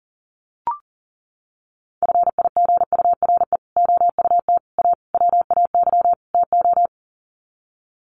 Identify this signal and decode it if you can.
Morse code, heard as a keyed tone, about 37 wpm, 700 Hz. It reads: FIGURE OUT A WAY TO
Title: Sets of 5 Words Encore2 40wpm